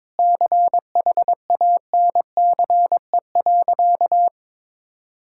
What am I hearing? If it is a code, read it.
Morse code, heard as a keyed tone, about 22 wpm, 700 Hz. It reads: CHANCE.
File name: practice / 22wpm / Sentences with 2-Word Phrases - 22wpm